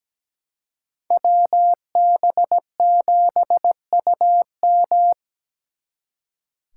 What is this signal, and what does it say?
Morse code, heard as a keyed tone, about 20 wpm, 700 Hz. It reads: WB7UM
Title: Speed Racing - USA 2x2 Call Signs 17wpm